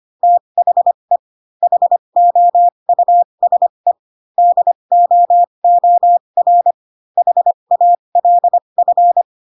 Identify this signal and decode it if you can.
Morse code, heard as a keyed tone, about 25 wpm, spaced as for 22 wpm, 700 Hz. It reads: THE HOUSE DOOR HALF
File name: Pinocchio / Pinocchio ch4 18 25 ik1hge